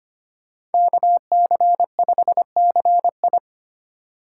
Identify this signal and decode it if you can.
Morse code, heard as a keyed tone, about 32 wpm, 700 Hz. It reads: KC5CI